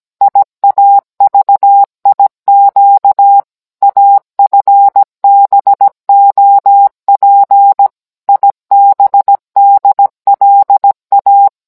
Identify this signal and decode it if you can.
Morse code, heard as a keyed tone, about 17 wpm, 800 Hz. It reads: IAVIQ AFBOP IBDLA